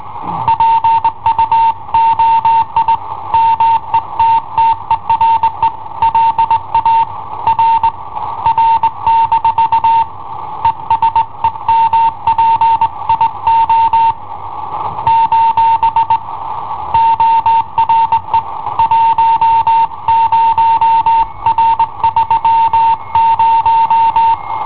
Morse code, heard as a keyed tone, about 19 wpm, 910 Hz. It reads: PUOI METTERE LA R R- ESEMPIO : ORE 10R30
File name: i7ohp / cassetta i7ohp B